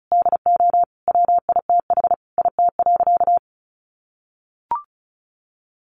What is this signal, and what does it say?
Morse code, heard as a keyed tone, about 34 wpm, 700 Hz. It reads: DO WITH IT.
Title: Sentences from Top 1100 Words 35wpm